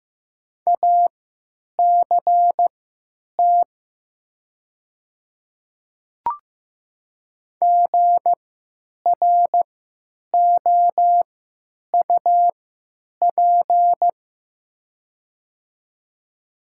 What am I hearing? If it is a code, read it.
Morse code, heard as a keyed tone, about 15 wpm, spaced as for 9 wpm, 700 Hz. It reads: ACT GROUP
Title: 3x Character Spacing - Top 300 Words - 15wpm